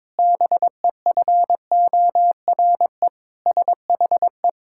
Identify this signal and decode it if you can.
Morse code, heard as a keyed tone, about 22 wpm, 700 Hz. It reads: BEFORE SHE
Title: Pangram Challenge v2 - 22wpm